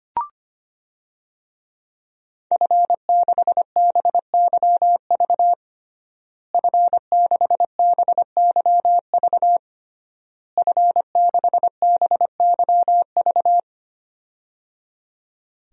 Morse code, heard as a keyed tone, about 25 wpm, 700 Hz. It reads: F6BYV F6BYV F6BYV
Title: DXCC-100 Call Signs - 5 Characters - Repeated 3x - 25wpm